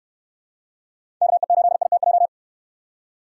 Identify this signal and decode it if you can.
Morse code, heard as a keyed tone, about 65 wpm, 700 Hz. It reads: KE8ITJ